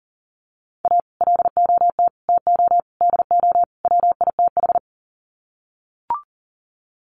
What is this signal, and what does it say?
Morse code, heard as a keyed tone, about 40 wpm, 700 Hz. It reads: A LOT TO DO WITH
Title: Sets of 5 Words 40wpm